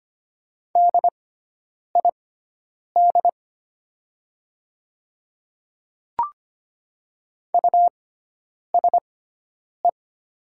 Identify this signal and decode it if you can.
Morse code, heard as a keyed tone, about 25 wpm, spaced as for 9 wpm, 700 Hz. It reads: DID USE
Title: 6x Character Spacing - Top 100 Words - 25wpm